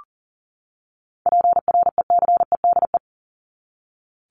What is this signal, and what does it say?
Morse code, heard as a keyed tone, about 40 wpm, 700 Hz. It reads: PRECEDE
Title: ING Suffix - Top 100 Words with Base Words 40wpm